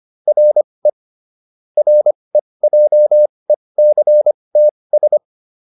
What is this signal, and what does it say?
Morse code, heard as a keyed tone, about 25 wpm, spaced as for 20 wpm, 600 Hz. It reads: RE REJECTS
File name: RE7 20